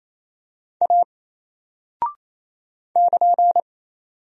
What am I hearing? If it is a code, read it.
Morse code, heard as a keyed tone, about 28 wpm, 700 Hz. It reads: A <KN>
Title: Single Character (Letter:Number:Punctuation:Prosign) 28wpm